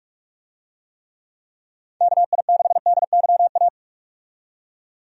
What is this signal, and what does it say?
Morse code, heard as a keyed tone, about 46 wpm, 700 Hz. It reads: KI6DYA
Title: USA Call Signs - 6 Characters - 45wpm